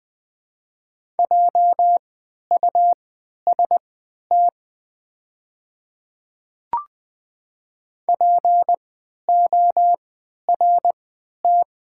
Morse code, heard as a keyed tone, about 20 wpm, spaced as for 11 wpm, 700 Hz. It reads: JUST PORT
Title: 3x Character Spacing - Top 200 Words - 20wpm